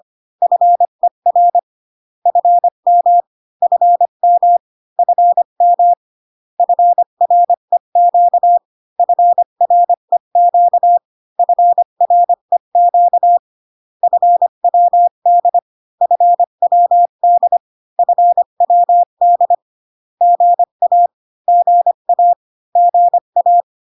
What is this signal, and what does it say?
Morse code, heard as a keyed tone, about 25 wpm, 700 Hz. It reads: FER FM FM FM FREQ FREQ FREQ FWD FWD FWD GA GA GA